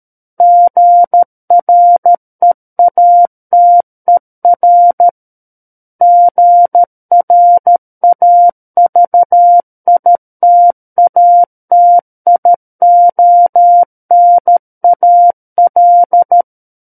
Morse code, heard as a keyed tone, about 13 wpm, 700 Hz. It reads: GREATER GRAVITATIONAL
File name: Book One / 04-Chapter 4 - 13 WPM